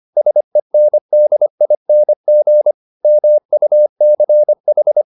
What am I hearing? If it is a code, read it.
Morse code, heard as a keyed tone, about 25 wpm, 600 Hz. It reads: SENDING MUCH